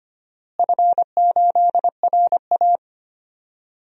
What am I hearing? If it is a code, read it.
Morse code, heard as a keyed tone, about 25 wpm, 700 Hz. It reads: F8RA
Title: DXCC-100 Call Signs - 4 Characters - Repeated 2x - 25wpm